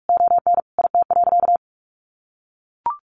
Morse code, heard as a keyed tone, about 44 wpm, 700 Hz. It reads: ON IT.
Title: Sentences from Top 1200 Words 45wpm